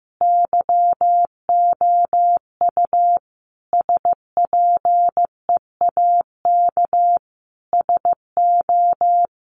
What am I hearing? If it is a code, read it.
Morse code, heard as a keyed tone, about 15 wpm, 700 Hz. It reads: YOU SPEAK SO
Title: Sentences from Top 1200 Words 15wpm